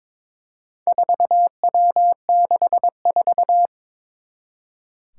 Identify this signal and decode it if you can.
Morse code, heard as a keyed tone, about 23 wpm, 700 Hz. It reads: 4W64
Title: Four Challenging Letter-Number Combinations 22wpm